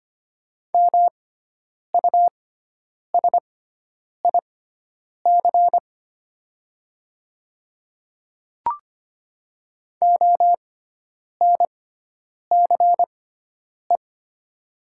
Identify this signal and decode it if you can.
Morse code, heard as a keyed tone, about 25 wpm, spaced as for 9 wpm, 700 Hz. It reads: MUSIC ONCE
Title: 6x Character Spacing - Top 300 Words - 25wpm